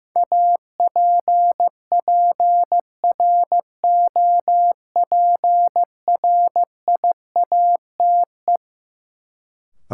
Morse code, heard as a keyed tone, about 16 wpm, 700 Hz. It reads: APPROPRIATE
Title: Sentences from Top 900 Words - New Word Drill - 15wpm